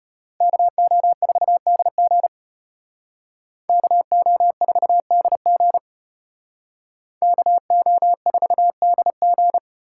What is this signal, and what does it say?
Morse code, heard as a keyed tone, about 32 wpm, 700 Hz. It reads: KO4DG KO4DG KO4DG